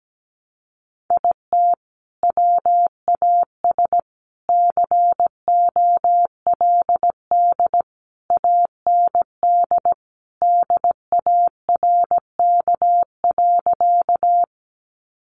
Morse code, heard as a keyed tone, about 17 wpm, 700 Hz. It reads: IT WAS COLD AND DARK.
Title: Sentences from Top 500 Words 17wpm